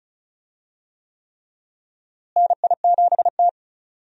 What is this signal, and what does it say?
Morse code, heard as a keyed tone, about 36 wpm, 700 Hz. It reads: NI7T
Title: USA Call Signs - 4 Characters - Repeated 3x - 35wpm